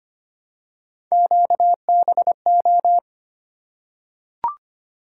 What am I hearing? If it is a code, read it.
Morse code, heard as a keyed tone, about 25 wpm, 700 Hz. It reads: QBO
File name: Three Letter-Number Combinations - 25wpm